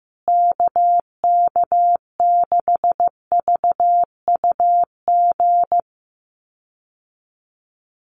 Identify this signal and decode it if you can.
Morse code, heard as a keyed tone, about 15 wpm, 700 Hz. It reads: KK6VUG
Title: USA Call Signs Repeated 3x 15wpm